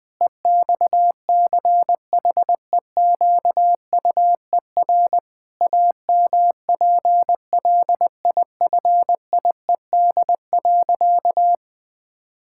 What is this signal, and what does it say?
Morse code, heard as a keyed tone, about 20 wpm, 700 Hz. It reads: EXCHEQUER AMPLIFIED.
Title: Pangram Challenge v2 - 20wpm